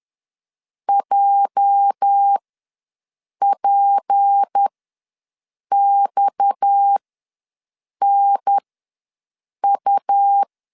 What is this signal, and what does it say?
Morse code, heard as a keyed tone, about 10 wpm, spaced as for 6 wpm, 800 Hz. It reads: JPXNU
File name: LEZIONI / 10 SIGNS / 10C DOT ws 10wpm 4213